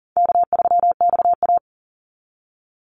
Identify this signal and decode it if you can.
Morse code, heard as a keyed tone, about 40 wpm, 700 Hz. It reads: K3XA
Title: Easy Call Signs - 40wpm